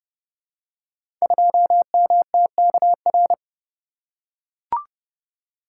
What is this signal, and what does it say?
Morse code, heard as a keyed tone, about 30 wpm, 700 Hz. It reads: 2MTKR
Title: Code Groups - 5 Letter:Numbers - 30wpm